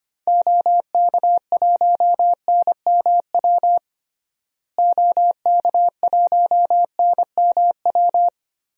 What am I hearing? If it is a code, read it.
Morse code, heard as a keyed tone, about 25 wpm, 690 Hz. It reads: OK1NMW OK1NMW
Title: DXCC-100 Call Signs - 6 Characters - Repeated 3x - 25wpm